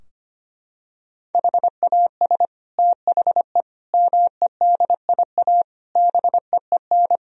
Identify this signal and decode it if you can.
Morse code, heard as a keyed tone, about 25 wpm, 700 Hz. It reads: HAS THE MEDIA BEEN
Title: Sentences from Top 1000 Words - Review Part 4 - 25wpm